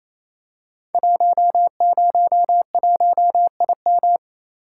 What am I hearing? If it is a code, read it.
Morse code, heard as a keyed tone, about 28 wpm, 700 Hz. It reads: 101IM
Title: Code Groups - 5 Letter:Numbers - 28wpm